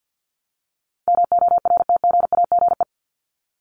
Morse code, heard as a keyed tone, about 50 wpm, 700 Hz. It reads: MORTGAGE
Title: Top 1751-2000 Words 50wpm